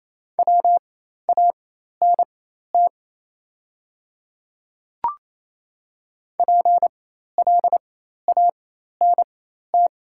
Morse code, heard as a keyed tone, about 28 wpm, spaced as for 13 wpm, 690 Hz. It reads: WANT PLANT
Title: 4x Character Spacing - Top 300 Words - 28wpm